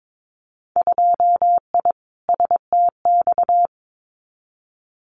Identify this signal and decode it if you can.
Morse code, heard as a keyed tone, about 22 wpm, 700 Hz. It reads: 2I STX
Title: Winter Field Day Exchanges 22wpm